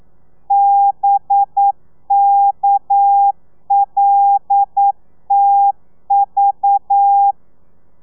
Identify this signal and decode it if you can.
Morse code, heard as a keyed tone, about 9 wpm, 800 Hz. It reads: BKLTV